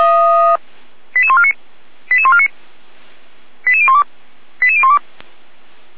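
Five-tone selective call (ZVEI), heard as quick runs of five tones, tones of about 70 ms. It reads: 88149 88149 88012 88012
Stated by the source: Five-tone ZVEI selective call received by SDR, from the Feuerwehr Orlamünde fire brigade audio server.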